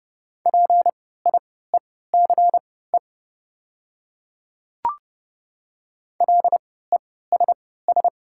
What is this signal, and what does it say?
Morse code, heard as a keyed tone, about 29 wpm, spaced as for 17 wpm, 700 Hz. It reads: PIECE LESS
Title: 3x Character Spacing - Top 500 Words - 30wpm